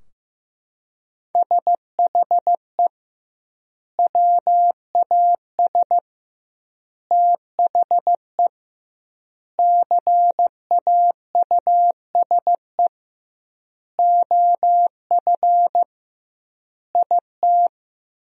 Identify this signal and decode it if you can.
Morse code, heard as a keyed tone, about 15 wpm, 700 Hz. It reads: SHE WAS THE CAUSE OF IT
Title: 2x Word-Spacing - Sentences from Top 200 Words - 15wpm